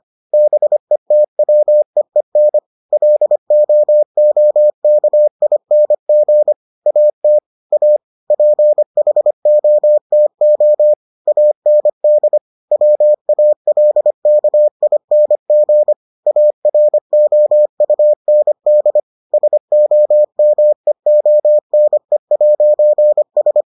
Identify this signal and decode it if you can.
Morse code, heard as a keyed tone, about 25 wpm, 600 Hz. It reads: BETWEEN LOOKING AT A PHOTO AND WALKING AROUND SOMEONE'S